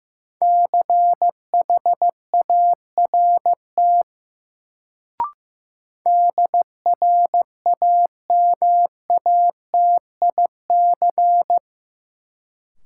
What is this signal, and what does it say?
Morse code, heard as a keyed tone, about 15 wpm, 700 Hz. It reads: CHART DRAMATIC
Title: Sentences from Top 1300 Words - New Word Drill Alt - 15wpm